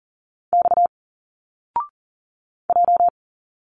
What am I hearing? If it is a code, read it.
Morse code, heard as a keyed tone, about 39 wpm, 700 Hz. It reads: X J
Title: Single Letters 40wpm